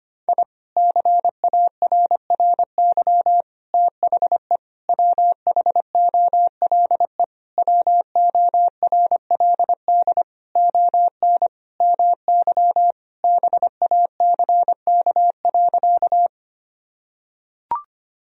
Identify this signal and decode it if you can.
Morse code, heard as a keyed tone, about 25 wpm, 700 Hz. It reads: I CARRY THE WHOLE WORLD ON MY BACK.
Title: Sentences from Top 1000 Words - Review Part 1 - 25wpm